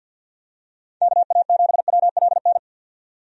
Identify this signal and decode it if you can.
Morse code, heard as a keyed tone, about 52 wpm, 700 Hz. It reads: KA7WLN
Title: USA Call Signs - 6 Characters - 50wpm